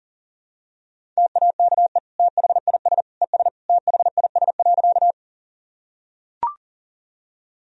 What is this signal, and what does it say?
Morse code, heard as a keyed tone, about 40 wpm, 700 Hz. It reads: TAKE THIS ES THIS.
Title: Sentences from Top 200 Words and Abbreviations 40wpm